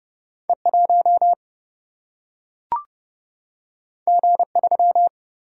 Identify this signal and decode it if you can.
Morse code, heard as a keyed tone, about 30 wpm, 700 Hz. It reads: E 1 G 3
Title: Single-Letter Single-Number 30wpm